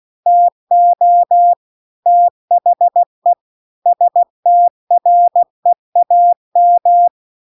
Morse code, heard as a keyed tone, about 16 wpm, 700 Hz. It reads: TO THE STREAM
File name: CW-The-Big-Twoheaded-River-2-16WPM